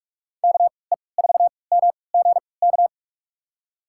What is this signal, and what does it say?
Morse code, heard as a keyed tone, about 46 wpm, spaced as for 26 wpm, 700 Hz. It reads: KE4MGK
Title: USA Call Signs - 6 Characters - 3x Character Spacing - 45wpm